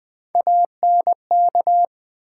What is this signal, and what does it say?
Morse code, heard as a keyed tone, about 20 wpm, 700 Hz. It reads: ANK
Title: Build A Word - 1 Letter At A Time - 3 to 5-Letter Words - 20wpm